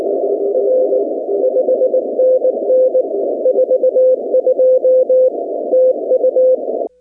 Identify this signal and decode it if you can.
Morse code, heard as a keyed tone, about 19 wpm, 500 Hz. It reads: R 5NN 42 TU
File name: contests / 2019-cw-wpx-kt5j-k5pi- -2330 0-17